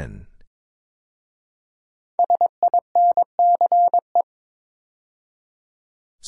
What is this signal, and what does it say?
Morse code, heard as a keyed tone, about 22 wpm, 690 Hz. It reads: SINCE
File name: 5 Letter Words - First 3 Letters Given - 22wpm